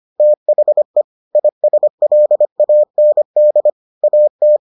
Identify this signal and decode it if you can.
Morse code, heard as a keyed tone, about 25 wpm, 600 Hz. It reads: THE ISLAND AT